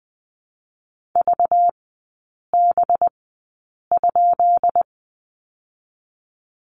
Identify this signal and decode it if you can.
Morse code, heard as a keyed tone, about 20 wpm, 700 Hz. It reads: V B ?
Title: 198 Lesson 32e - Letter k - Three Characters 20wpm